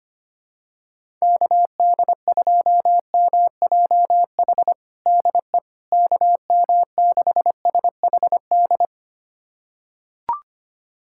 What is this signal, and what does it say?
Morse code, heard as a keyed tone, about 25 wpm, 700 Hz. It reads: KD2MJH DE KM6SHD
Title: USA Callsign DE Callsign 25wpm